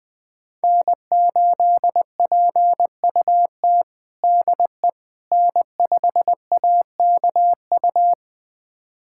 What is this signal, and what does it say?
Morse code, heard as a keyed tone, about 21 wpm, 700 Hz. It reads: N8PUT DE N5AKU
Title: USA Callsign DE Callsign - 5 Characters - 20wpm